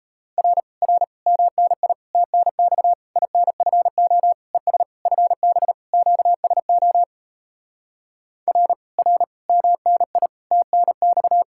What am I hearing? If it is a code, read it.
Morse code, heard as a keyed tone, about 38 wpm, 700 Hz. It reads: R R MNI TNX INFO ES FB QSO R R MNI TNX